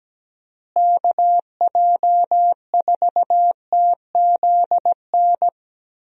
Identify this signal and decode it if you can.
Morse code, heard as a keyed tone, about 17 wpm, 700 Hz. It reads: KJ4TZN
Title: USA Call Signs - 6 Characters - Repeated 2x - 17wpm